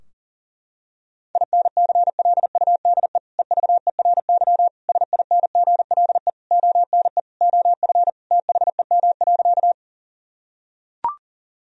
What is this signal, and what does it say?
Morse code, heard as a keyed tone, about 37 wpm, 700 Hz. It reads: INCLUDE EVERY SINGLE ONE OF THEM.
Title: Sentences from Top 1000 Words - Review Part 4 - 40wpm